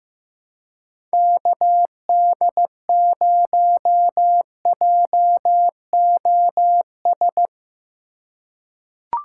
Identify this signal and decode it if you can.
Morse code, heard as a keyed tone, about 15 wpm, 700 Hz. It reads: KD0JOS
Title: USA Call Signs - 4x Character Spacing - 15wpm